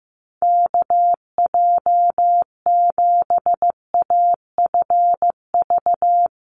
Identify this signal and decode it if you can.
Morse code, heard as a keyed tone, about 15 wpm, 700 Hz. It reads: KJ7AFV
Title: USA Call Signs 15wpm